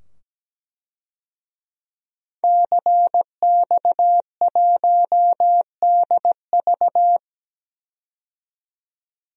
Morse code, heard as a keyed tone, about 17 wpm, 700 Hz. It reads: CX1DV
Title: DXCC-100 Call Signs - 5 Characters - 17wpm